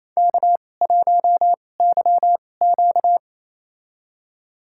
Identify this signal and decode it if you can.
Morse code, heard as a keyed tone, about 28 wpm, spaced as for 20 wpm, 700 Hz. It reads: K1YQ
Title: USA Call Signs - 4 Characters - 2x Character Spacing - 28wpm